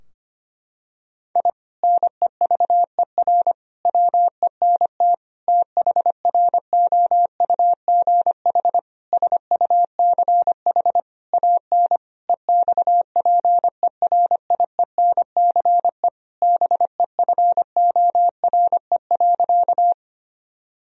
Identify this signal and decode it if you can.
Morse code, heard as a keyed tone, about 25 wpm, 700 Hz. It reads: I NEVER WENT THROUGH SUCH AN EXPERIENCE BEFORE.
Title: Sentences from Top 1000 Words - Review Part 3 - 25wpm